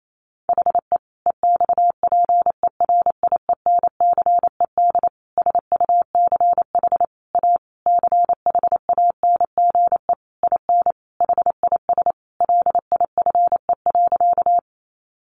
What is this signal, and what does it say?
Morse code, heard as a keyed tone, about 28 wpm, 700 Hz. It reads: HE EXPERIENCED SUCH A CHANGE IN HIS LIFE.
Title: Sentences from Top 1100 Words 28wpm